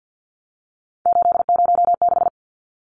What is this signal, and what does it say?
Morse code, heard as a keyed tone, about 50 wpm, 700 Hz. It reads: 806
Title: Numbers - 3-Digit 50wpm